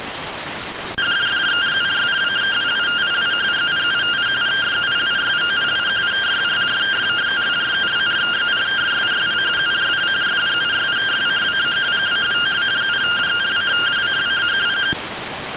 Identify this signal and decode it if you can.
Two-tone teleprinter signal, about 170 Hz shift.